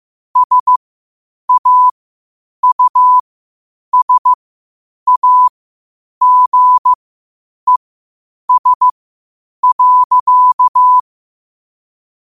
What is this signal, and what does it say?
Morse code, heard as a keyed tone, about 15 wpm, spaced as for 8 wpm, 1000 Hz. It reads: SAUSAGES.
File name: qotdfiles / 8wpm